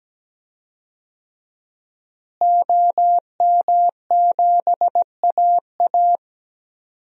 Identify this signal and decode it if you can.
Morse code, heard as a keyed tone, about 17 wpm, 700 Hz. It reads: OM7AA
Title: DXCC-100 Call Signs - 5 Characters - Repeated 3x - 17wpm